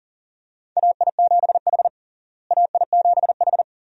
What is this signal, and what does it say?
Morse code, heard as a keyed tone, about 40 wpm, 700 Hz. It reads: AI7H AI7H
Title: USA Call Signs - 4 Characters - Repeated 2x - 40wpm